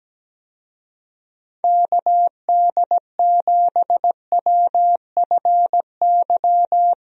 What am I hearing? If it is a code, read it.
Morse code, heard as a keyed tone, about 17 wpm, 700 Hz. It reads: KD7WFY
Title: USA Call Signs - 6 Characters - Repeated 2x - 17wpm